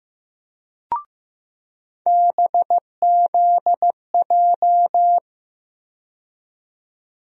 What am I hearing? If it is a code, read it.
Morse code, heard as a keyed tone, about 16 wpm, 700 Hz. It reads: BZJ
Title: USA Call Sign Postfixes - 3 Letters 15wpm